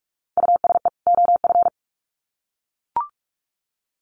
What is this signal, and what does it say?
Morse code, heard as a keyed tone, about 44 wpm, 700 Hz. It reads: USE OF
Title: Sets of 2 Words 45wpm